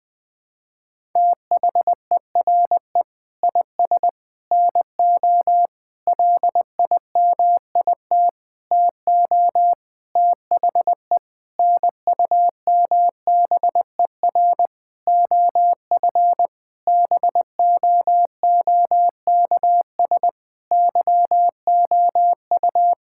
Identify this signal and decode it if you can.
Morse code, heard as a keyed tone, about 20 wpm, 700 Hz. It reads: THERE IS NO LIMIT TO THE NUMBER OF BOOKS YOU
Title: Sentences with 8-Word Phrases - Spoken Beginning - 20wpm